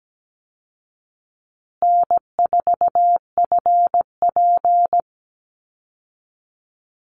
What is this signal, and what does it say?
Morse code, heard as a keyed tone, about 17 wpm, 700 Hz. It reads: N4FP
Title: Call Signs Repeated 2x - 17wpm